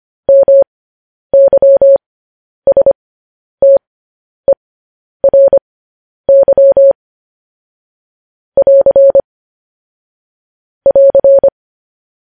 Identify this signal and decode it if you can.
Morse code, heard as a keyed tone, about 25 wpm, spaced as for 10 wpm, 550 Hz. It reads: MYSTERY <AR> <AR>
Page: News Headlines in Morse Code at 10 WPM